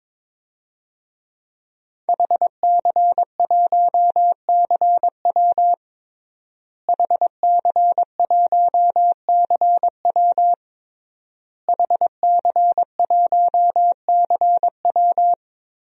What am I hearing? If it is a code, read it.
Morse code, heard as a keyed tone, about 22 wpm, 700 Hz. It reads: HC1CW HC1CW HC1CW
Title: DXCC-100 Call Signs - 5 Characters - Repeated 3x - 22wpm